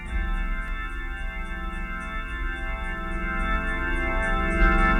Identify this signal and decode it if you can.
Emergency Alert System (SAME) tones, heard as harsh buzzing bursts. It is heard from 1.1 to 4.2 s.
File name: wrct-log hi- 16